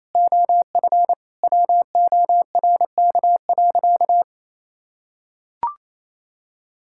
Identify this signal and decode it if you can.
Morse code, heard as a keyed tone, about 28 wpm, 700 Hz. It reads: OF WORK.
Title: Sentences from Top 1000 Words - Review Part 6 - 28wpm